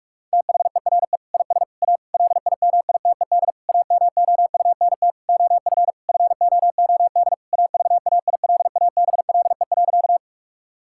Morse code, heard as a keyed tone, about 46 wpm, 700 Hz. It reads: THERE IS A LIMITED AMOUNT OF FOOD AVAILABLE.
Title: Sentences from Top 1000 Words - Review Part 6 - 45wpm